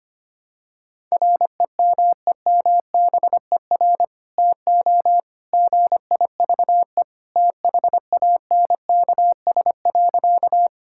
Morse code, heard as a keyed tone, about 25 wpm, 700 Hz. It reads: REMEMBER TO GIVE THANKS.
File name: practice / 25wpm / Sentences with 2-Word Phrases - 25wpm